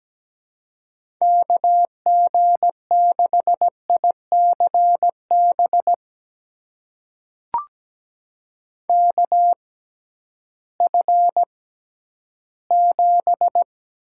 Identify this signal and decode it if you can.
Morse code, heard as a keyed tone, about 17 wpm, spaced as for 11 wpm, 700 Hz. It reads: KG6ICB KF7
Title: USA Call Signs - 6 Characters - 6x Character Spacing - 17wpm